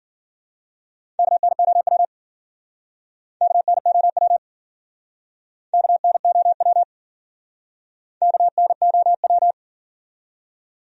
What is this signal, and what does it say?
Morse code, heard as a keyed tone, about 52 wpm, 700 Hz. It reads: KNOW KNOW KNOW KNOW